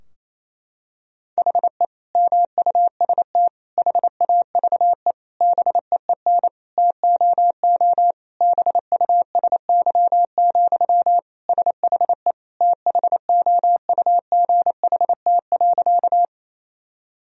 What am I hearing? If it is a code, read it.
Morse code, heard as a keyed tone, about 28 wpm, 700 Hz. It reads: HE MUST HAVE BEEN TOO BUSY, SHE THOUGHT.
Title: Sentences from Top 1000 Words - Review Part 3 - 28wpm